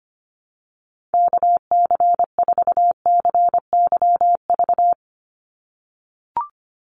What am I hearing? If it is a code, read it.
Morse code, heard as a keyed tone, about 25 wpm, 700 Hz. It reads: KC4CYV
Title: USA Call Signs 25wpm